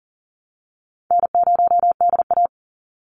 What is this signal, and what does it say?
Morse code, heard as a keyed tone, about 40 wpm, 700 Hz. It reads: N0DA
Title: NA SOTA Chaser Callsigns 40wpm